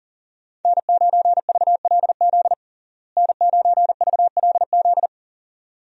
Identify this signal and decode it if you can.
Morse code, heard as a keyed tone, about 40 wpm, 700 Hz. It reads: N9VLZ N9VLZ
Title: USA Call Signs - 5 Characters - Repeated 2x - 40wpm